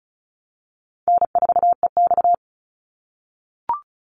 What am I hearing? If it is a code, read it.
Morse code, heard as a keyed tone, about 34 wpm, 700 Hz. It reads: N4EX
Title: NA SOTA Activator Callsigns 35wpm